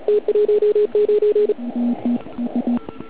Morse code, heard as a keyed tone, about 36 wpm, 420 Hz. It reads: T19
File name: Archiv / CUC129 / CUC129 pileup